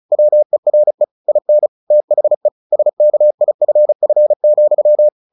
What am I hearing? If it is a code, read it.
Morse code, heard as a keyed tone, about 34 wpm, 600 Hz. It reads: WERE IN THE SKIFF,
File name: the old man and the sea 2 35wpm